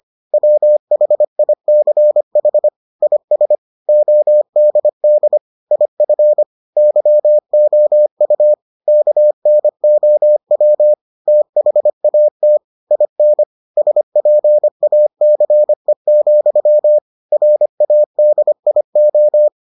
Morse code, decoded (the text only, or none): WHICH IS ODD IF YOU KNOW THAT IN SPACE, RADIO